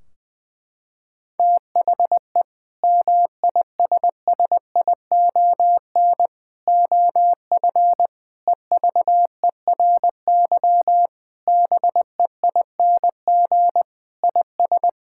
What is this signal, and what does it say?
Morse code, heard as a keyed tone, about 20 wpm, 700 Hz. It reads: THE MISSION OF EVERY BEING IS
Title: Sentences from Top 1600 Words 20wpm